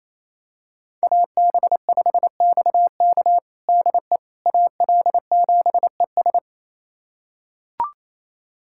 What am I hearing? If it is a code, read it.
Morse code, heard as a keyed tone, about 28 wpm, 700 Hz. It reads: AB5XK DE AL7ES
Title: USA Callsign DE Callsign - 5 Characters - 28wpm